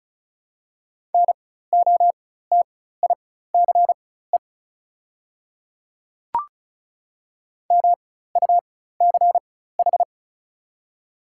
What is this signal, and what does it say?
Morse code, heard as a keyed tone, about 34 wpm, spaced as for 16 wpm, 700 Hz. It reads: NOTICE MUCH
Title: 4x Character Spacing - Top 500 Words - 35wpm